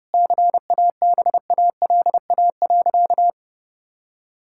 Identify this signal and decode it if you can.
Morse code, heard as a keyed tone, about 32 wpm, 700 Hz. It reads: CABALA.
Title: Pangram Challenge v2 - 30wpm